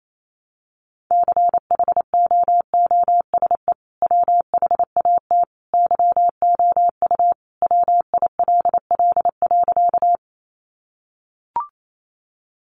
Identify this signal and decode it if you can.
Morse code, heard as a keyed tone, about 28 wpm, 700 Hz. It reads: CHOOSE WHAT YOU WILL.
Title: Sentences from Top 700 Words 28wpm